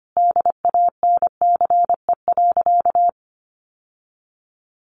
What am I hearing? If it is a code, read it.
Morse code, heard as a keyed tone, about 25 wpm, 700 Hz. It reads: DANCE.
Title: Sentences from Top 700 Words 25wpm